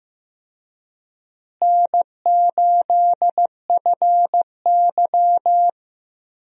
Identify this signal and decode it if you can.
Morse code, heard as a keyed tone, about 15 wpm, 700 Hz. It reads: N8FY
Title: USA Call Signs - 4 Characters - Repeated 3x - 15wpm